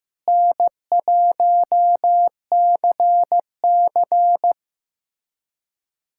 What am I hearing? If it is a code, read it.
Morse code, heard as a keyed tone, about 15 wpm, 700 Hz. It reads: N1CC
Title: USA Callsign DE Callsign - 4 Characters - 15wpm